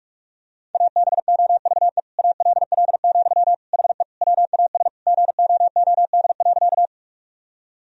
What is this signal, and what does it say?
Morse code, heard as a keyed tone, about 46 wpm, 700 Hz. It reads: ABOVE ALL, HE WAS GOOD.
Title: Sentences from Top 1000 Words - Review Part 2 - 45wpm